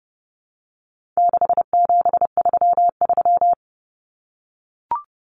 Morse code, heard as a keyed tone, about 30 wpm, 700 Hz. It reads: 6733
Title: Numbers - 4-Digit 30wpm